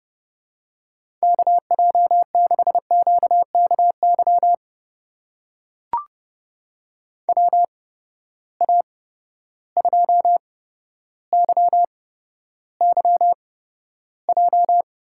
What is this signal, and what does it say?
Morse code, heard as a keyed tone, about 30 wpm, spaced as for 13 wpm, 700 Hz. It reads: KJ6QKY WA2YYJ
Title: USA Call Signs - 8x Character Spacing - 30wpm